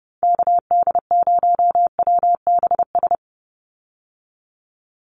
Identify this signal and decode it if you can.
Morse code, heard as a keyed tone, about 30 wpm, 700 Hz. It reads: KD0WBS
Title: USA Call Signs 30wpm